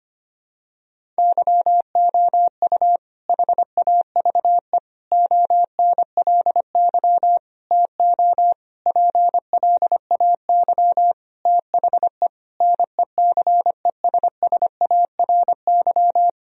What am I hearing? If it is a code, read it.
Morse code, heard as a keyed tone, about 25 wpm, 700 Hz. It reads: YOU HAVE ONLY TO PLAY THE NECESSARY
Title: Sentences from Top 1000 Words - Review Part 1 - 25wpm